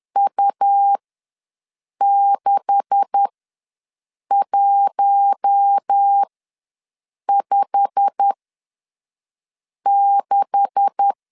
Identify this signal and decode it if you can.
Morse code, heard as a keyed tone, about 10 wpm, spaced as for 5 wpm, 800 Hz. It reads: U6156
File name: LEZIONI / 9 286 ws 10wpm 4032